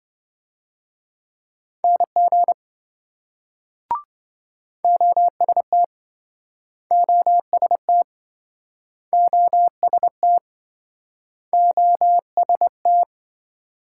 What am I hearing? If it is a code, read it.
Morse code, heard as a keyed tone, about 28 wpm, 700 Hz. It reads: NG OST OST OST OST